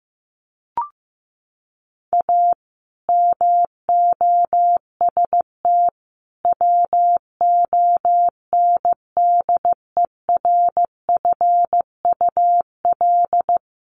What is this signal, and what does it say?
Morse code, heard as a keyed tone, about 15 wpm, 700 Hz. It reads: A MOST WONDERFUL
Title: Sentences from Top 1100 Words 15wpm